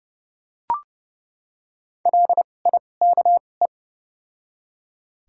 Morse code, heard as a keyed tone, about 30 wpm, spaced as for 22 wpm, 700 Hz. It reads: LIKE
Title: 2x Character Spacing - Top 200 Words - 30wpm